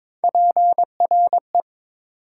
Morse code, heard as a keyed tone, about 22 wpm, 700 Hz. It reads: PRE
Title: Prefixes and Suffixes 22wpm